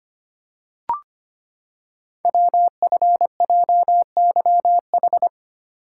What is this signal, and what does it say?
Morse code, heard as a keyed tone, about 25 wpm, 700 Hz. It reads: WFJYH
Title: Code Groups - 5 Letters - 25wpm